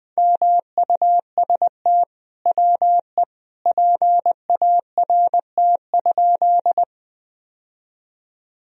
Morse code, heard as a keyed tone, about 20 wpm, 700 Hz. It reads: MUST WE PART?
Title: Sentences from Top 1000 Words - Review Part 1 - 20wpm